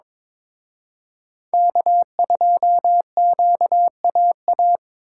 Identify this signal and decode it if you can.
Morse code, heard as a keyed tone, about 26 wpm, 700 Hz. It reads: K2QAA
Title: Speed Racing - USA 1x3 Call Signs 22wpm